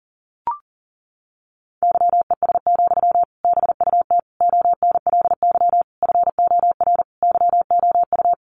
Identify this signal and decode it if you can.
Morse code, heard as a keyed tone, about 37 wpm, 700 Hz. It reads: YES, BUT ONLY FOR YOU
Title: Sentences from Top 500 Words 40wpm